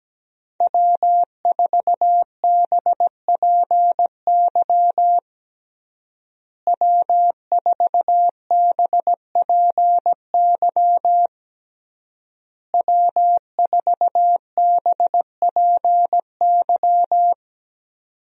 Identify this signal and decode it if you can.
Morse code, heard as a keyed tone, about 17 wpm, 700 Hz. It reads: W4BPY W4BPY W4BPY
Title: USA Call Signs - 5 Characters - Repeated 3x - 17wpm